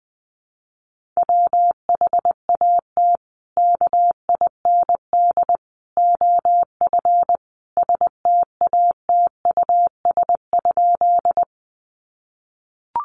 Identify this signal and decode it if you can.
Morse code, heard as a keyed tone, about 20 wpm, 700 Hz. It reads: WHAT KIND OF STATUS?
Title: Sentences from Top 1400 Words 20wpm